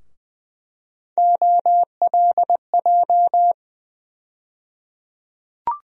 Morse code, heard as a keyed tone, about 20 wpm, 700 Hz. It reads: OLJ
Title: USA Call Sign Postfixes - 3 Letters 20wpm